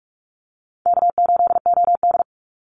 Morse code, heard as a keyed tone, about 44 wpm, 700 Hz. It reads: K8OD
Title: Call Signs Repeated 3x - 45wpm